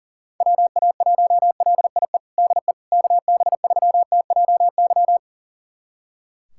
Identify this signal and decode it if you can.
Morse code, heard as a keyed tone, about 40 wpm, 700 Hz. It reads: WA1LIE DE KB3TJY
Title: USA Callsign DE Callsign - 6 Characters - 40wpm